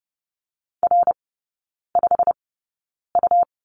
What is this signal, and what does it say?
Morse code, heard as a keyed tone, about 30 wpm, spaced as for 9 wpm, 700 Hz. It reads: R5U
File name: Lesson 18e - Letter m - Three Characters 30wpm